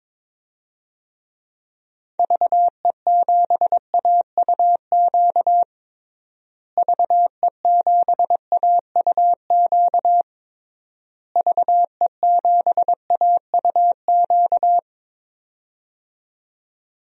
Morse code, heard as a keyed tone, about 22 wpm, 700 Hz. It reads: VE7AUQ VE7AUQ VE7AUQ
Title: DXCC-100 Call Signs - 6 Characters - Repeated 3x - 22wpm